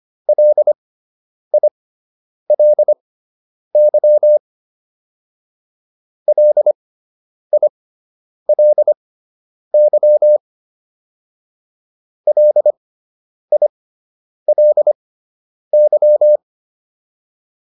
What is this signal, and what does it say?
Morse code, heard as a keyed tone, about 25 wpm, spaced as for 9 wpm, 600 Hz. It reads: LILY LILY LILY